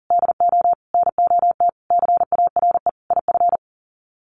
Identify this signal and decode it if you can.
Morse code, heard as a keyed tone, about 40 wpm, 700 Hz. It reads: DO NOT CARE IF
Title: Sets of 5 Words 40wpm